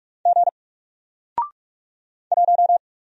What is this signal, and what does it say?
Morse code, heard as a keyed tone, about 48 wpm, 700 Hz. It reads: G 1
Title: Single Letter-Number Alt 45wpm